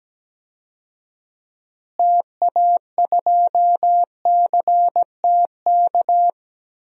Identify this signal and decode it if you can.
Morse code, heard as a keyed tone, about 17 wpm, 700 Hz. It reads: TA2CTK
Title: DXCC-100 Call Signs - 6 Characters - Repeated 3x - 17wpm